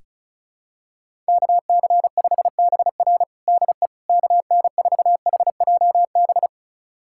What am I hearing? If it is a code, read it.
Morse code, heard as a keyed tone, about 34 wpm, 700 Hz. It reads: KC5BR DE KN4HJB
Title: USA Callsign DE Callsign 35wpm